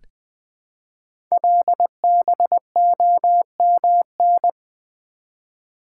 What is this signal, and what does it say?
Morse code, heard as a keyed tone, about 21 wpm, 700 Hz. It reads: LBOMN
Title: Code Groups - 5 Letters - 20wpm